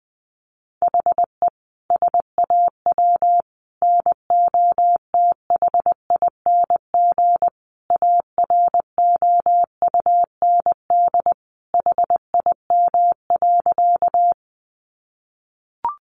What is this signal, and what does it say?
Morse code, heard as a keyed tone, about 19 wpm, 700 Hz. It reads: HE SAW NOTHING AROUND HIM.
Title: Sentences from Top 600 Words 20wpm